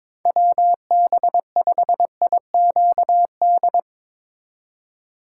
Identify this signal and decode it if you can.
Morse code, heard as a keyed tone, about 22 wpm, 700 Hz. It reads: WB5IQD
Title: USA Call Signs - 8x Character Spacing - 22wpm